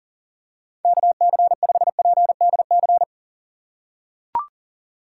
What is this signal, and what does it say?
Morse code, heard as a keyed tone, about 40 wpm, 700 Hz. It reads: KC5PDC
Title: USA Call Signs - 6 Characters - 6x Character Spacing - 40wpm